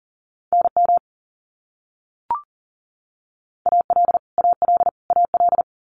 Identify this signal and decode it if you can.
Morse code, heard as a keyed tone, about 40 wpm, 700 Hz. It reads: NM AL AL AL
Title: US States Repeated 3x 40wpm